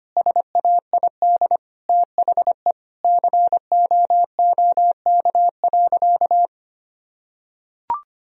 Morse code, heard as a keyed tone, about 25 wpm, 700 Hz. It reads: SAID THE COOK.
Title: Sentences from Top 1000 Words - Review Part 4 - 25wpm